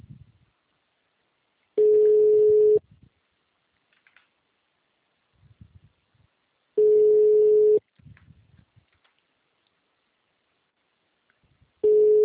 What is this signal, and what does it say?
Ringback tone (Europe, 425 Hz).